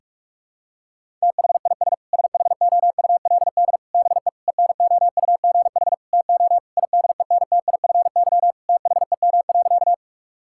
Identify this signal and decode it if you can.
Morse code, heard as a keyed tone, about 41 wpm, 700 Hz. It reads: THIS SHOULD BE ENOUGH TO IDENTIFY THEM.
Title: Sentences from Top 1000 Words - Review Part 6 - 45wpm